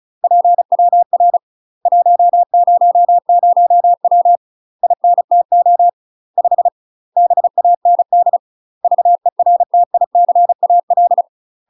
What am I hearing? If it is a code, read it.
Morse code, heard as a keyed tone, about 35 wpm, 700 Hz. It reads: PWR 100W INTO 5 BAND VERTICAL
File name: UK MASTER QSOs TRAINING / 35WPM / UK MASTER QSOs 0033